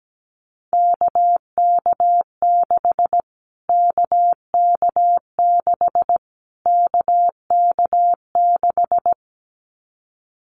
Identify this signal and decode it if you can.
Morse code, heard as a keyed tone, about 17 wpm, 700 Hz. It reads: KK6 KK6 KK6
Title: Callsign Prefixes for Top 25 DXCC - Repeated 3x - 17wpm